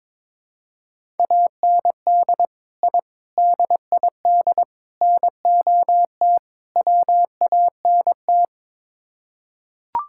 Morse code, heard as a keyed tone, about 22 wpm, 700 Hz. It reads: AND I DID NOT WANT
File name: practice / 22wpm / Build A Phrase - 5-Word Phrases - 22wpm